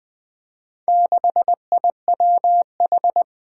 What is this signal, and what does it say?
Morse code, heard as a keyed tone, about 21 wpm, 700 Hz. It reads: 6IWH
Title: Four Challenging Letter-Number Combinations 20wpm